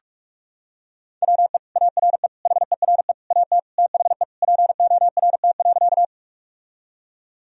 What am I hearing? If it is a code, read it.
Morse code, heard as a keyed tone, about 46 wpm, 700 Hz. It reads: WE ARE HERE AT THE PORT.
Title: Sentences from Top 1000 Words - Review Part 1 - 45wpm